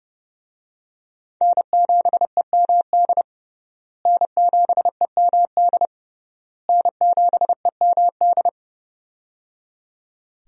Morse code, heard as a keyed tone, about 30 wpm, 700 Hz. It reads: N7EMD N7EMD N7EMD
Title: USA Call Signs - 5 Characters - Repeated 3x - 30wpm